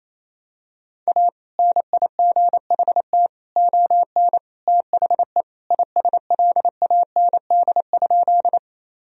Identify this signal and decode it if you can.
Morse code, heard as a keyed tone, about 27 wpm, 700 Hz. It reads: A NIGHT ON THE ISLAND?
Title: Sentences from Top 1000 Words - Review Part 3 - 28wpm